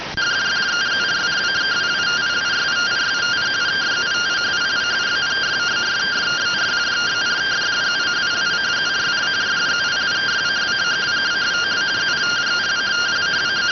Two-tone teleprinter signal, about 170 Hz shift.